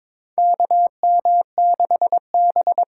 Morse code, heard as a keyed tone, about 21 wpm, 700 Hz. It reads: KM6B
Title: USA Call Signs - 6 Characters - 6x Character Spacing - 22wpm